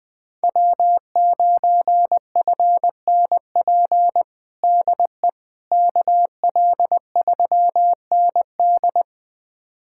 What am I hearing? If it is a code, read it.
Morse code, heard as a keyed tone, about 21 wpm, 700 Hz. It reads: W9FNP DE KL3ND
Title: USA Callsign DE Callsign - 5 Characters - 20wpm